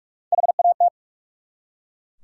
Morse code, heard as a keyed tone, about 44 wpm, 700 Hz. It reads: HAT
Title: Three Letter Words - 45wpm